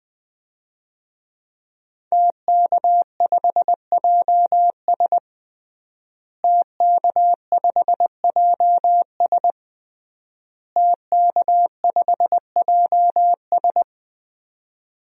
Morse code, heard as a keyed tone, about 20 wpm, 700 Hz. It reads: TK5JS TK5JS TK5JS
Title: DXCC-100 Call Signs - 5 Characters - Repeated 3x - 20wpm